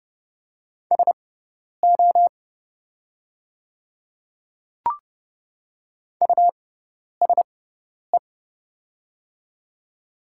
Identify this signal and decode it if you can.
Morse code, heard as a keyed tone, about 30 wpm, spaced as for 10 wpm, 700 Hz. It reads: SO USE
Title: 6x Character Spacing - Top 200 Words - 30wpm